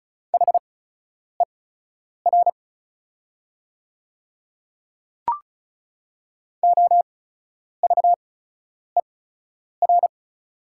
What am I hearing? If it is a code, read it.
Morse code, heard as a keyed tone, about 36 wpm, spaced as for 10 wpm, 700 Hz. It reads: HER OVER
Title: 8x Character Spacing - Top 300 Words - 35wpm